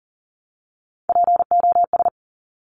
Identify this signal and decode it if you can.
Morse code, heard as a keyed tone, about 42 wpm, 700 Hz. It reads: POS
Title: Top Three Letter Combinations in Words 40wpm